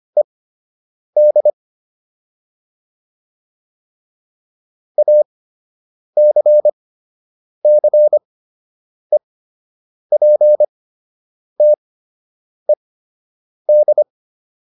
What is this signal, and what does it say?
Morse code, heard as a keyed tone, about 25 wpm, spaced as for 8 wpm, 600 Hz. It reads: ED ACCEPTED